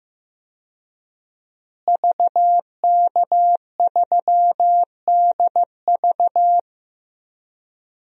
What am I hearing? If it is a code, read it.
Morse code, heard as a keyed tone, about 15 wpm, 700 Hz. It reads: VK3DV
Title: DXCC-100 Call Signs - 5 Characters - Repeated 2x - 15wpm